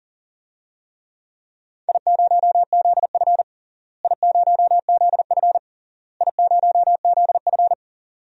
Morse code, heard as a keyed tone, about 40 wpm, 700 Hz. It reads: I0ZF I0ZF I0ZF
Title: DXCC-100 Call Signs - 4 Characters - Repeated 3x - 40wpm